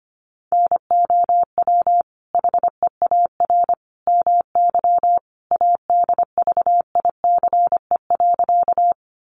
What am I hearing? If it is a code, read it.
Morse code, heard as a keyed tone, about 25 wpm, 700 Hz. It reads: NOW HEAR MY ADVICE.
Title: Sentences from Top 1100 Words 25wpm